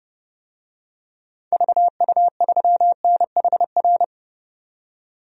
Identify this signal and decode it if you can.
Morse code, heard as a keyed tone, about 30 wpm, 700 Hz. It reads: VU3NHR
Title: DXCC-100 Call Signs - 6 Characters - 30wpm